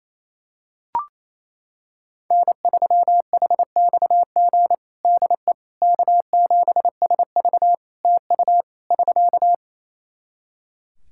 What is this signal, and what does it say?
Morse code, heard as a keyed tone, about 28 wpm, 700 Hz. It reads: N3HXG DE K7SV TU <SK>
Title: K1USN SST QSO Practice 28wpm